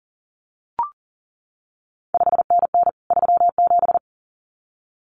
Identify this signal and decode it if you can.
Morse code, heard as a keyed tone, about 41 wpm, 700 Hz. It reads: 5NN 37
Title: CQ WW DX Contest Exchange Encore - 40wpm